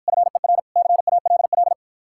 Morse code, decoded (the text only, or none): FER CALL